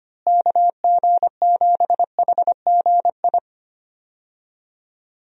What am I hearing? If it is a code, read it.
Morse code, heard as a keyed tone, about 25 wpm, 700 Hz. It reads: KG7HGI
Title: USA Call Signs - 6 Characters - Repeated 2x - 25wpm